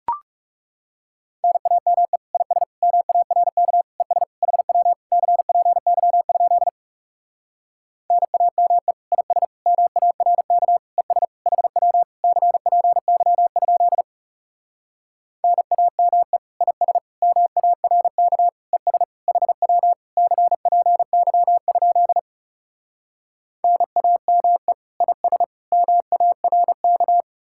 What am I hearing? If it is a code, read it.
Morse code, heard as a keyed tone, about 38 wpm, 700 Hz. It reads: NAME IS MARK ES HW CPY? NAME IS MARK ES HW CPY? NAME IS MARK ES HW CPY? NAME IS MARK